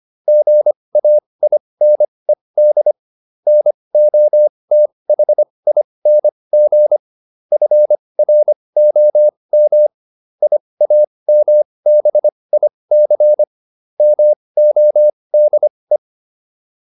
Morse code, heard as a keyed tone, about 25 wpm, spaced as for 20 wpm, 600 Hz. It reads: GAINED NOTHING FROM IAMBIC MODE